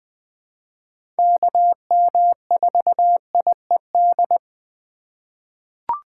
Morse code, heard as a keyed tone, about 20 wpm, 700 Hz. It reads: KM4IED
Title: USA Call Signs - 6 Characters - 4x Character Spacing - 20wpm